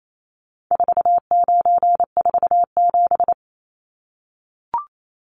Morse code, heard as a keyed tone, about 28 wpm, 700 Hz. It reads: 4947
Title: Numbers - 4-Digit 28wpm